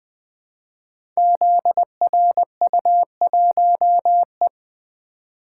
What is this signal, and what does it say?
Morse code, heard as a keyed tone, about 20 wpm, 700 Hz. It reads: ZRU1E
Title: Code Groups - 5 Letter:Numbers - 20wpm